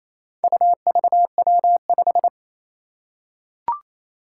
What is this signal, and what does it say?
Morse code, heard as a keyed tone, about 29 wpm, 700 Hz. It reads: UVW5
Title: Four Challenging Letter-Number Combinations 28wpm